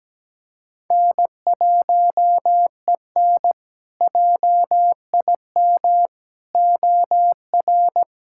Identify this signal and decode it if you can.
Morse code, heard as a keyed tone, about 17 wpm, 700 Hz. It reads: N1EN JIM OR
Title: K1USN SST QSO Practice - 17wpm